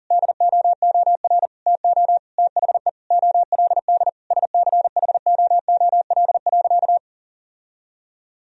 Morse code, heard as a keyed tone, about 40 wpm, 700 Hz. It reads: DOOR TO THE OLD SCHOOL.
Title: Sentences from Top 1000 Words - Review Part 1 - 40wpm